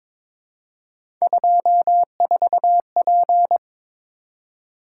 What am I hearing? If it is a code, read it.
Morse code, heard as a keyed tone, about 22 wpm, 700 Hz. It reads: 24P
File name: practice / 22wpm / Three Letter-Number Combinations - 22wpm